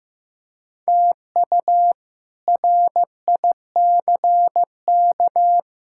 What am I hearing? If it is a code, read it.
Morse code, heard as a keyed tone, about 15 wpm, 700 Hz. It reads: TU RICK
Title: K1USN SST QSO Practice - 15wpm